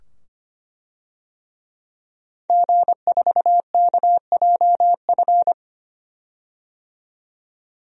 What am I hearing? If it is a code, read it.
Morse code, heard as a keyed tone, about 25 wpm, 700 Hz. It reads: G4KJF
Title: DXCC-100 Call Signs - 5 Characters - 25wpm